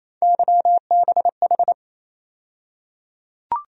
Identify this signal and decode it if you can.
Morse code, heard as a keyed tone, about 28 wpm, 700 Hz. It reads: YBH
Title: USA Call Sign Postfixes - 3 Letters 28wpm